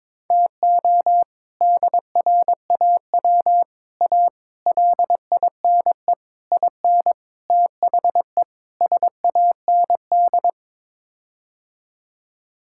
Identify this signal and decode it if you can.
Morse code, heard as a keyed tone, about 22 wpm, 700 Hz. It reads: TO DRAW A LINE IN THE SAND
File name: practice / 22wpm / Sets of 7 Words - 22wpm